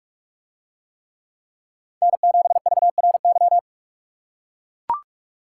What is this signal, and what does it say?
Morse code, heard as a keyed tone, about 44 wpm, 700 Hz. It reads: N7VRY